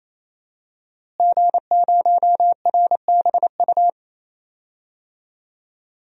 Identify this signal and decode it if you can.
Morse code, heard as a keyed tone, about 28 wpm, 700 Hz. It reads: G0RBU
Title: DXCC-100 Call Signs - 5 Characters - Repeated 3x - 28wpm